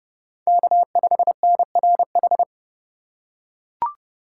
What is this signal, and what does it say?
Morse code, heard as a keyed tone, about 27 wpm, 700 Hz. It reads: K5NRH